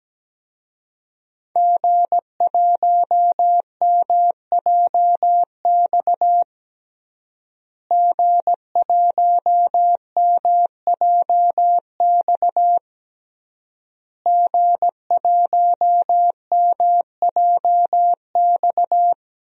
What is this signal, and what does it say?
Morse code, heard as a keyed tone, about 17 wpm, 700 Hz. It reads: G1MJX G1MJX G1MJX
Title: DXCC-100 Call Signs - 5 Characters - Repeated 3x - 17wpm